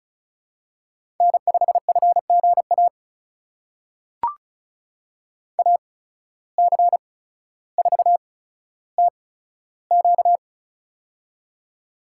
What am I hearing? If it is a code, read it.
Morse code, heard as a keyed tone, about 36 wpm, spaced as for 15 wpm, 700 Hz. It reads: N5FGA AC4TQ
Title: USA Call Signs - 5 Characters - 8x Character Spacing - 35wpm